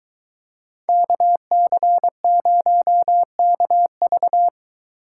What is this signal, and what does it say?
Morse code, heard as a keyed tone, about 23 wpm, 700 Hz. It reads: KC0KV